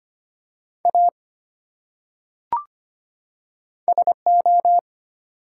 Morse code, heard as a keyed tone, about 25 wpm, 700 Hz. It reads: A SO
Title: Top 180 QSO Elements 25wpm